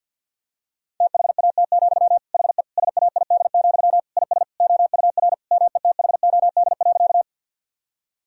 Morse code, heard as a keyed tone, about 50 wpm, 700 Hz. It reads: THAT, HE SAID, IS OUR METHOD.
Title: Sentences from Top 1000 Words - Review Part 3 - 50wpm